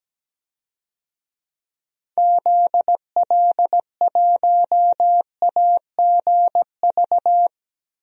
Morse code, heard as a keyed tone, about 17 wpm, 700 Hz. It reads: ZL1AGV
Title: DXCC-100 Call Signs - 6 Characters - Repeated 3x - 17wpm